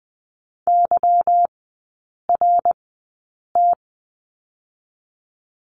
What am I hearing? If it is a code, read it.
Morse code, heard as a keyed tone, about 20 wpm, 690 Hz. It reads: Y R T
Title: 145 Lesson 24e - Letter y - Three Characters 20wpm